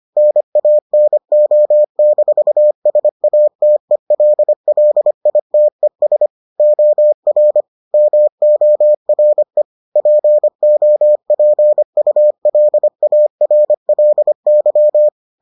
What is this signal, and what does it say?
Morse code, heard as a keyed tone, about 25 wpm, 600 Hz. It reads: NANO-SATELLITES OR MORE POPULARLY